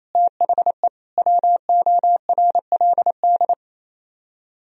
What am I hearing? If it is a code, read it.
Morse code, heard as a keyed tone, about 26 wpm, 700 Hz. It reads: THE WORLD
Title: Speed-Racing - Sets of 3 Words 28wpm